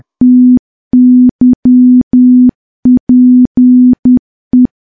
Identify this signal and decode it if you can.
Morse code, heard as a keyed tone, about 10 wpm, 250 Hz. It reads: TYPE
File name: morse / morse-training-10WPM